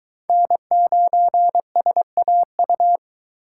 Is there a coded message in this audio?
Yes: Morse code from 0.3 to 3.0 s, heard as a keyed tone, about 23 wpm, 700 Hz. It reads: N9SAU